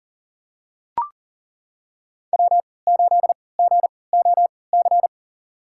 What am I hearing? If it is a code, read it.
Morse code, heard as a keyed tone, about 40 wpm, spaced as for 23 wpm, 700 Hz. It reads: W8GOC
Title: USA Call Signs - 5 Characters - 3x Character Spacing - 40wpm